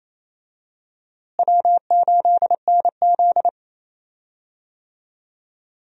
Morse code, heard as a keyed tone, about 28 wpm, 700 Hz. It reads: W8NZ
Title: USA Call Signs - 4 Characters - Repeated 3x - 28wpm